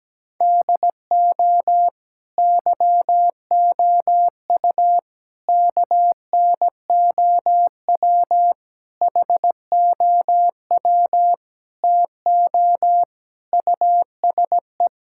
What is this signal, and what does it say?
Morse code, heard as a keyed tone, about 17 wpm, 700 Hz. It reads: DO YOU KNOW HOW TO USE
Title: Sentences from Top 1000 Words - Review Part 6 - 17wpm